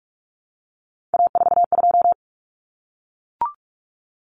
Morse code, heard as a keyed tone, about 44 wpm, 700 Hz. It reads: A42
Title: Common Cut Numbers - 3-Digit 45wpm